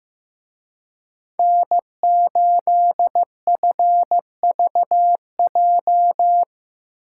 Morse code, heard as a keyed tone, about 17 wpm, 700 Hz. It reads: N8FVJ
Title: Speed Racing - USA Call Signs 15wpm